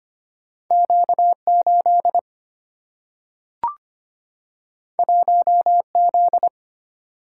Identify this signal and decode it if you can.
Morse code, heard as a keyed tone, about 25 wpm, 700 Hz. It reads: Q 8 1 Z
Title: Single-Letter Single-Number 25wpm